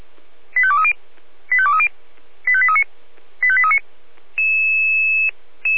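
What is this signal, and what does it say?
Five-tone selective call (ZVEI), heard as quick runs of five tones, tones of about 70 ms. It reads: 85230 85230 85839 85839
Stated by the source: ZVEI five-tone call 85830, received by SDR.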